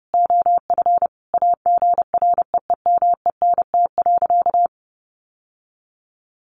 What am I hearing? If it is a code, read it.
Morse code, heard as a keyed tone, about 30 wpm, 700 Hz. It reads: OF AGREEMENT.
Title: Sentences from Top 1500 Words 30wpm